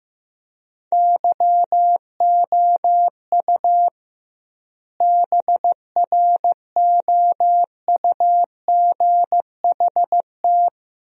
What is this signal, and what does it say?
Morse code, heard as a keyed tone, about 15 wpm, 700 Hz. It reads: YOU BROUGHT
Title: 2x Word-Spacing - Sentences from Top 500 Words - 15wpm